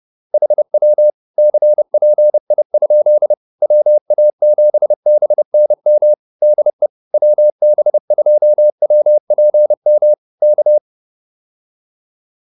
Morse code, heard as a keyed tone, about 30 wpm, 600 Hz. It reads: HW CPI? WA7BNM DE WB2WPM K